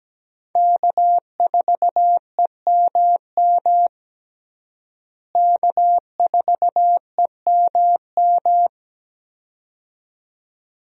Morse code, heard as a keyed tone, about 17 wpm, 700 Hz. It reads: K4EMM K4EMM
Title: USA Call Signs - 5 Characters - Repeated 2x - 17wpm